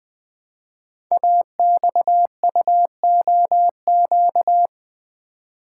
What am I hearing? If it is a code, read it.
Morse code, heard as a keyed tone, about 20 wpm, 700 Hz. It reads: AXUOQ
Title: Code Groups - 5 Letters - 20wpm